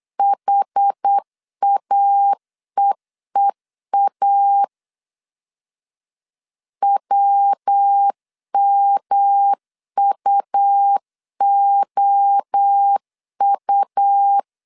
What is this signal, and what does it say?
Morse code, heard as a keyed tone, about 8 wpm, 800 Hz. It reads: HAEEA WMUOU
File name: CW / DET / 1 3 1500C S B 8wpm 4141